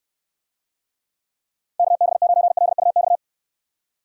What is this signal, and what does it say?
Morse code, heard as a keyed tone, about 69 wpm, 700 Hz. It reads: KB9LUX